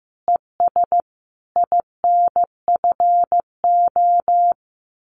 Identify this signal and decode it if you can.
Morse code, heard as a keyed tone, about 15 wpm, 700 Hz. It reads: ES INFO
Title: Common QSO Phrases Encore 15wpm